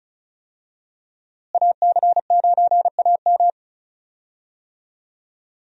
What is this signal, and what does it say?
Morse code, heard as a keyed tone, about 36 wpm, 700 Hz. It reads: AC9AM
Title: USA Call Signs - 5 Characters - Repeated 2x - 35wpm